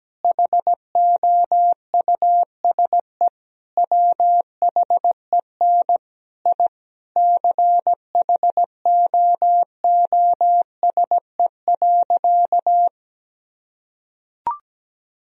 Morse code, heard as a keyed tone, about 17 wpm, 700 Hz. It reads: HOUSE WHEN I CHOOSE.
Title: Sentences from Top 1000 Words - Review Part 4 - 17wpm